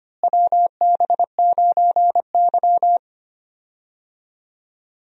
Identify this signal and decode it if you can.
Morse code, heard as a keyed tone, about 25 wpm, 700 Hz. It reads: WB9Y
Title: USA Call Signs - 4 Characters - Repeated 3x - 25wpm